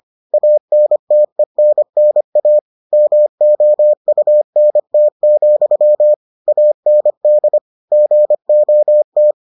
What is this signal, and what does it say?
Morse code, heard as a keyed tone, about 25 wpm, 600 Hz. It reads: ANTENNA MOUNT, AND GOT